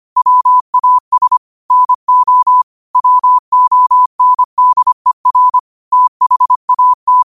Morse code, heard as a keyed tone, about 25 wpm, 1000 Hz. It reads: WAS NO WONDER THAT